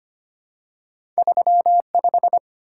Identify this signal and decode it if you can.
Morse code, heard as a keyed tone, about 25 wpm, 700 Hz. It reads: 35
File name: Two Letter-Number Combinations - 25wpm